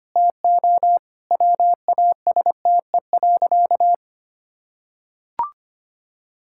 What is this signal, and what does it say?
Morse code, heard as a keyed tone, about 25 wpm, 700 Hz. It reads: TO WASTE.
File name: practice / 25wpm / Sentences with 3-Word Phrases - 25wpm